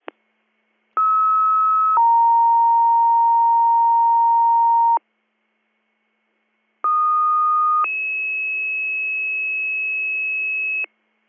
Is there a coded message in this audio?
Yes: two-tone page from 1.0 to 10.9 s, heard as a one-second tone, then a three-second tone, about 1290 then 930 Hz, about 1230 then 2600 Hz.